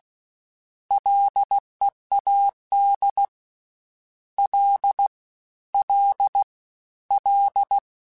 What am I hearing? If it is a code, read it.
Morse code, heard as a keyed tone, about 16 wpm, 800 Hz. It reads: LEAD L L L
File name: Prefixes 3x2 Twice @15